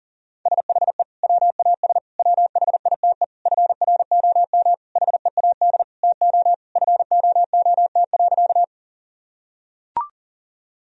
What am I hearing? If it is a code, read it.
Morse code, heard as a keyed tone, about 37 wpm, 700 Hz. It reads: SHE WAS WHITE FROM HEAD TO FOOT.
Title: Sentences from Top 1000 Words - Review Part 3 - 40wpm